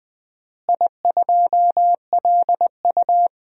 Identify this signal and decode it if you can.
Morse code, heard as a keyed tone, about 20 wpm, 700 Hz. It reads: I2LU
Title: DXCC-100 Call Signs - 4 Characters - Repeated 2x - 20wpm